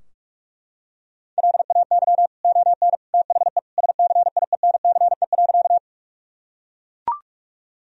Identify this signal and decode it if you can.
Morse code, heard as a keyed tone, about 41 wpm, 700 Hz. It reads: LAY ON THE SCIENCE.
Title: Sentences from Top 1000 Words - Review Part 2 - 45wpm